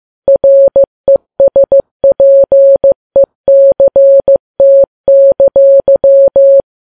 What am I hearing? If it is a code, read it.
Morse code, heard as a keyed tone, about 15 wpm, 550 Hz. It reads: RESPECT!
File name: Files Web tls / Morse on9fv / morse15kps